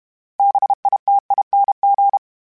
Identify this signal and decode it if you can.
Morse code, heard as a keyed tone, about 32 wpm, 800 Hz. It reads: BITING
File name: Special INGs-mp3s / Special INGs-mp3s / Sngl Ing @30